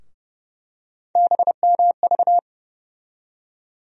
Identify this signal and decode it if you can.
Morse code, heard as a keyed tone, about 30 wpm, 700 Hz. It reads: BMV
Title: USA Call Sign Postfixes - 3 Letters 30wpm